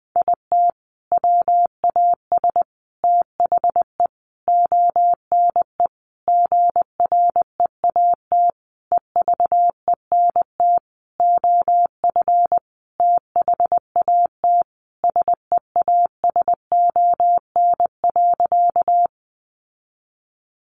Morse code, heard as a keyed tone, about 20 wpm, 700 Hz. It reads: IT WAS THE ONE GREAT EVENT OF THAT SEASON.
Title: Sentences from Top 1400 Words 20wpm